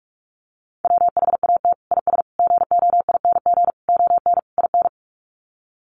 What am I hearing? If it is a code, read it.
Morse code, heard as a keyed tone, about 44 wpm, 700 Hz. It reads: WHAT IS GOING ON IN
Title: Sets of 5 Words Encore2 45wpm